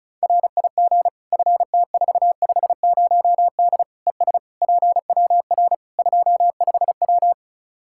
Morse code, heard as a keyed tone, about 34 wpm, 700 Hz. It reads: RIG FT450D ES PWR 25W
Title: K1USN SST QSO Practice 35wpm